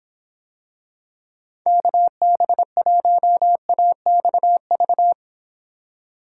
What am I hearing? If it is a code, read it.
Morse code, heard as a keyed tone, about 26 wpm, 700 Hz. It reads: KB1AXV